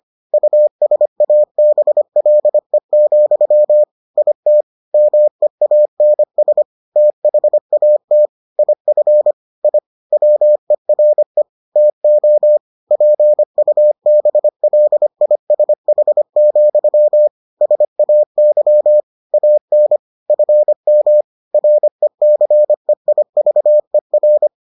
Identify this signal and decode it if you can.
Morse code, heard as a keyed tone, about 25 wpm, 600 Hz. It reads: USABLE, IT MEANS THAT IF I WERE TO PUBLISH, SAY AN FM RECEIVER